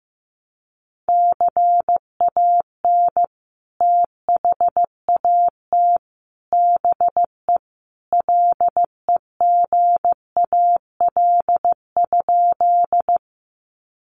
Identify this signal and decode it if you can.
Morse code, heard as a keyed tone, about 15 wpm, 700 Hz. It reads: CAN THAT BE LEGAL?
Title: Sentences from Top 1200 Words 15wpm